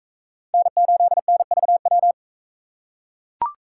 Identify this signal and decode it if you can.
Morse code, heard as a keyed tone, about 38 wpm, 700 Hz. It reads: N8NVW